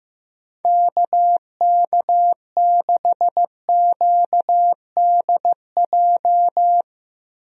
Morse code, heard as a keyed tone, about 15 wpm, 700 Hz. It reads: KK6QDJ
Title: USA Call Signs - 6 Characters - Repeated 3x - 15wpm